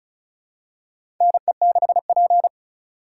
Morse code, heard as a keyed tone, about 36 wpm, 700 Hz. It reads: NE6P
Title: USA Call Signs - 4 Characters - 2x Character Spacing - 35wpm